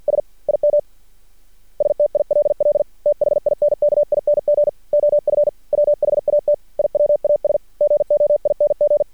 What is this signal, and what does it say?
Morse code, heard as a keyed tone, about 51 wpm, 590 Hz. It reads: S IM STILL THINKING OF WHAT I WAS GOING
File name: archived / CRSnet